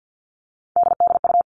Morse code, heard as a keyed tone, about 50 wpm, 700 Hz. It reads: DDU
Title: All Cut Numbers - 3-Digit 50wpm